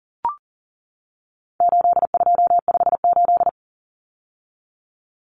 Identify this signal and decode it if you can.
Morse code, heard as a keyed tone, about 40 wpm, 700 Hz. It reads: 8258
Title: Common Cut Numbers - 4-Digit 40wpm